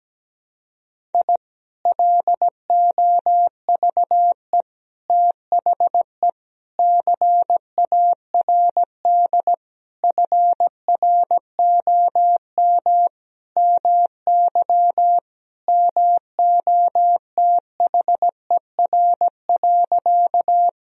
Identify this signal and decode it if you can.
Morse code, heard as a keyed tone, about 17 wpm, 700 Hz. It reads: I LOVE THE CARD FROM MY MOTHER.
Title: Sentences from Top 1000 Words - Review Part 4 - 17wpm